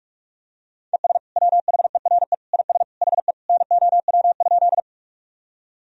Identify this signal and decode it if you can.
Morse code, heard as a keyed tone, about 45 wpm, 700 Hz. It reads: ES WHERE IS HE NOW?
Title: Sentences from Top 200 Words and Abbreviations 45wpm